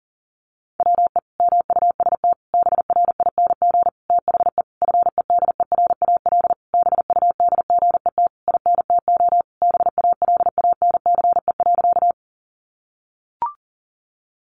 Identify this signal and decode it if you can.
Morse code, heard as a keyed tone, about 37 wpm, 700 Hz. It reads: WE MUST BRING THE FEDERAL BUDGET INTO BALANCE.
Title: Sentences from Top 1200 Words 40wpm